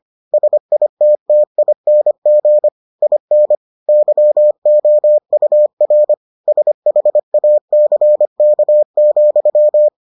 Morse code, heard as a keyed tone, about 25 wpm, 600 Hz. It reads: SITTING IN YOUR SHACK,